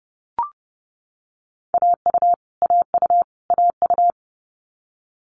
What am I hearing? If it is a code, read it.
Morse code, heard as a keyed tone, about 30 wpm, 700 Hz. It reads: AU AU AU
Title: Multi-letter Phonemes Repeated 3x 30wpm